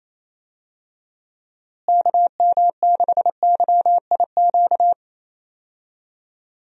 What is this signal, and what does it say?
Morse code, heard as a keyed tone, about 27 wpm, 700 Hz. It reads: KM6YIQ
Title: USA Call Signs - 6 Characters - Repeated 2x - 28wpm